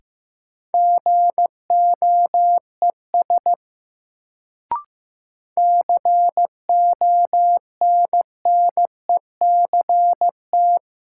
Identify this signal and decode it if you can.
Morse code, heard as a keyed tone, about 15 wpm, 700 Hz. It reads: GOES CONNECT
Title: Sentences from Top 1500 Words - New Word Drill Alt - 15wpm